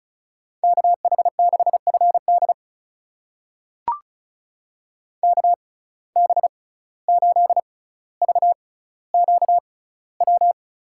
Morse code, heard as a keyed tone, about 34 wpm, spaced as for 17 wpm, 700 Hz. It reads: KH6FD KB8VQW
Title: USA Call Signs - 6x Character Spacing - 35wpm